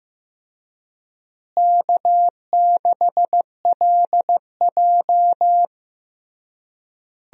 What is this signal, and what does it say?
Morse code, heard as a keyed tone, about 15 wpm, 700 Hz. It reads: K6LJ
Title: USA Call Signs - 4 Characters - Repeated 3x - 15wpm